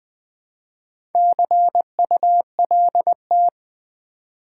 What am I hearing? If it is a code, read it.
Morse code, heard as a keyed tone, about 20 wpm, 700 Hz. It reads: CULT
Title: Build A Word - 1 Letter At A Time - 3 to 5-Letter Words - 20wpm